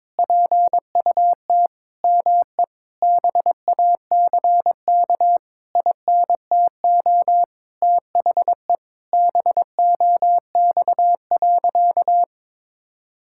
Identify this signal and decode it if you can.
Morse code, heard as a keyed tone, about 22 wpm, 700 Hz. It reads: PUT ME BACK INTO THE BOX.
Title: Sentences from Top 1000 Words - Review Part 6 - 22wpm